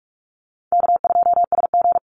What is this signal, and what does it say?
Morse code, heard as a keyed tone, about 44 wpm, 700 Hz. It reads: K2SG
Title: Call Signs Repeated 2x - 45wpm